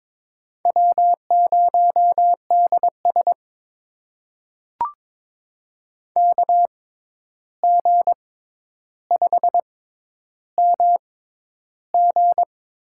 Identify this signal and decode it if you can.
Morse code, heard as a keyed tone, about 22 wpm, spaced as for 11 wpm, 700 Hz. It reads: W0DS KG5MG
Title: USA Call Signs - 6x Character Spacing - 22wpm